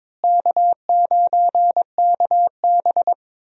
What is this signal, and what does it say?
Morse code, heard as a keyed tone, about 22 wpm, 700 Hz. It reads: K9KB
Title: USA Call Signs - 4 Characters - Repeated 3x - 22wpm